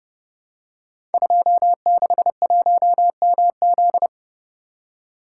Morse code, heard as a keyed tone, about 30 wpm, 700 Hz. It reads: 261MZ
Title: Code Groups - 5 Letter:Numbers - 30wpm